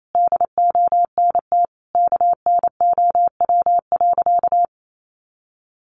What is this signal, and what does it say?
Morse code, heard as a keyed tone, about 28 wpm, 700 Hz. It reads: DONT KNOW.
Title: Sentences from Top 600 Words 28wpm